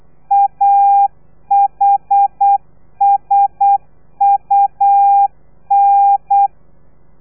Morse code, heard as a keyed tone, about 8 wpm, 800 Hz. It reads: AHSUN
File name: telegrafia / i2tfj / Lez3 35